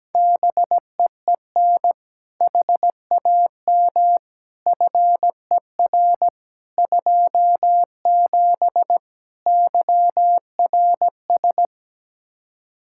Morse code, heard as a keyed tone, about 17 wpm, 700 Hz. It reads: BEEN HAM FER 27 YRS
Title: Practice QSO - Exchange 3 - More Info - 17wpm